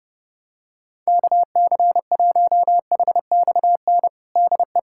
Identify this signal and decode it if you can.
Morse code, heard as a keyed tone, about 30 wpm, 700 Hz. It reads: KC1HXN DE
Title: USA Callsign DE Callsign - 6 Characters - 30wpm